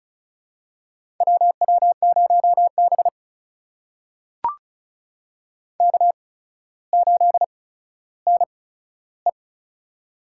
Morse code, heard as a keyed tone, about 36 wpm, spaced as for 15 wpm, 700 Hz. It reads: WW0B K8NE
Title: USA Call Signs - 4 Characters - 8x Character Spacing - 35wpm